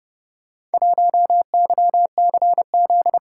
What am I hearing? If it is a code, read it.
Morse code, heard as a keyed tone, about 28 wpm, 700 Hz. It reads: 1YCZ